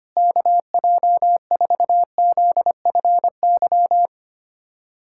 Morse code, heard as a keyed tone, about 25 wpm, 700 Hz. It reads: KJ4ZFY
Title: USA Callsign DE Callsign - 6 Characters - 25wpm